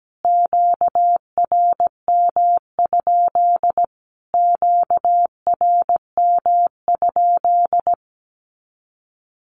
Morse code, heard as a keyed tone, about 17 wpm, 700 Hz. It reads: QRM? QRM?
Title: Q-Codes Repeated 3x 17wpm